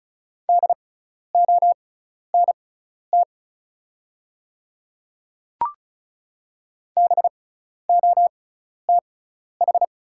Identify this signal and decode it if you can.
Morse code, heard as a keyed tone, about 36 wpm, spaced as for 12 wpm, 690 Hz. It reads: DONT BOTH
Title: 6x Character Spacing - Top 300 Words - 35wpm